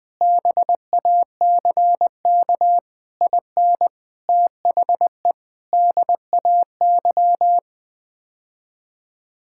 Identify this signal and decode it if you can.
Morse code, heard as a keyed tone, about 20 wpm, 700 Hz. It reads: BACK IN THE DAY
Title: Sentences with 3-Word Phrases - 20wpm